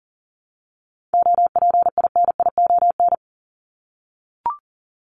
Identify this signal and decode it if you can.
Morse code, heard as a keyed tone, about 40 wpm, 700 Hz. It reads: OPINION
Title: Top 1001-1250 Words 40wpm